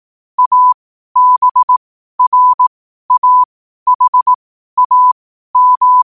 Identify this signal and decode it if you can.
Morse code, heard as a keyed tone, about 18 wpm, spaced as for 12 wpm, 1000 Hz. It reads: ABRAHAM
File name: qotdfiles / 12wpm